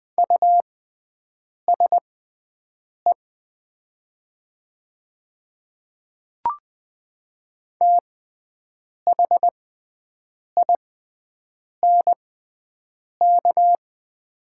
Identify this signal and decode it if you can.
Morse code, heard as a keyed tone, about 20 wpm, spaced as for 7 wpm, 700 Hz. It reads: USE THINK
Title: 6x Character Spacing - Top 500 Words - 20wpm